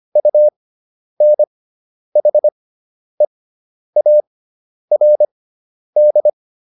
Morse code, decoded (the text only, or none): UNHEARD